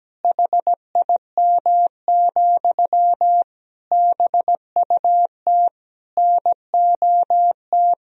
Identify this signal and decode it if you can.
Morse code, heard as a keyed tone, about 17 wpm, 700 Hz. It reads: HIM, BUT NOT
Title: Sentences from Top 1000 Words - Review Part 4 - 17wpm